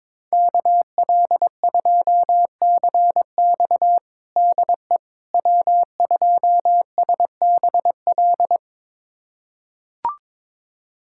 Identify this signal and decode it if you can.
Morse code, heard as a keyed tone, about 22 wpm, 700 Hz. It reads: KL2CX DE W2SBL
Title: USA Callsign DE Callsign - 5 Characters - 22wpm